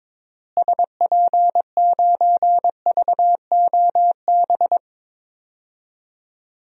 Morse code, heard as a keyed tone, about 22 wpm, 700 Hz. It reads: SP9VOB
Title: DXCC-100 Call Signs - 6 Characters - Repeated 2x - 22wpm